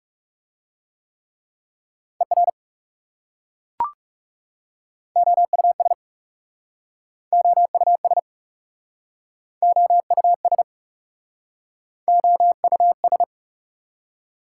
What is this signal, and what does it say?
Morse code, heard as a keyed tone, about 40 wpm, 700 Hz. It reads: ER OUS OUS OUS OUS